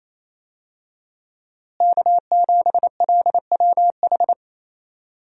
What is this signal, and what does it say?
Morse code, heard as a keyed tone, about 28 wpm, 700 Hz. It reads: K7LWH
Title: USA Call Signs - 5 Characters - Repeated 3x - 28wpm